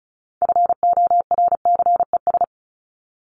Morse code, heard as a keyed tone, about 34 wpm, 700 Hz. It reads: FORCES
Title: ES:S Suffix - Top 100 Words with Base Words 35wpm